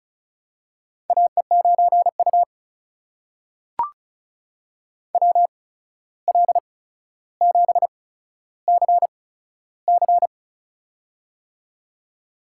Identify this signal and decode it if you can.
Morse code, heard as a keyed tone, about 36 wpm, spaced as for 14 wpm, 700 Hz. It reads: AE9U WL7CC
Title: USA Call Signs - 8x Character Spacing - 35wpm